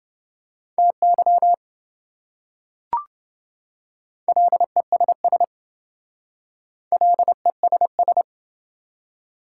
Morse code, heard as a keyed tone, about 28 wpm, 700 Hz. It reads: TY LESS LESS